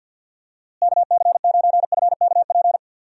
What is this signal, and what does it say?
Morse code, heard as a keyed tone, about 48 wpm, 700 Hz. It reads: KC9FKP
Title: USA Call Signs - 6 Characters - Repeated 2x - 50wpm